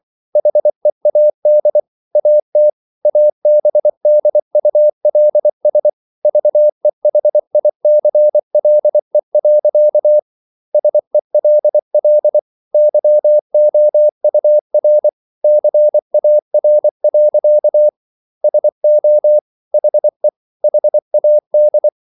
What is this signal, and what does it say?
Morse code, heard as a keyed tone, about 24 wpm, 600 Hz. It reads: HEAD AT ABDULS VEHICLE. SELL YOUR CAR. SO HE HAD